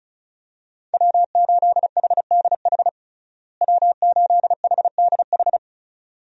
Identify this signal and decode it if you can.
Morse code, heard as a keyed tone, about 36 wpm, 700 Hz. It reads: W8HDH W8HDH
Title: USA Call Signs - 5 Characters - Repeated 2x - 35wpm